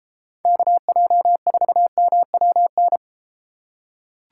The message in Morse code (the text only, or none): KJ4MWN